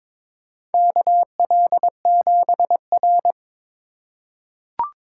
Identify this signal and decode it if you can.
Morse code, heard as a keyed tone, about 22 wpm, 700 Hz. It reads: KL7R
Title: USA Call Signs - 4 Characters - 2x Character Spacing - 22wpm